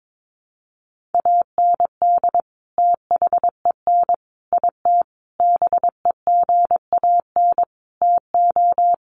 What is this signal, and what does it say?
Morse code, heard as a keyed tone, about 22 wpm, 700 Hz. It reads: AND THEN IT BEGAN TO
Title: Sentences from Top 1200 Words 22wpm